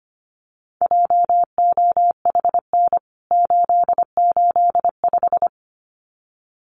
Morse code, heard as a keyed tone, about 25 wpm, 700 Hz. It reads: JOHN 885
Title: CWT Exchange 25wpm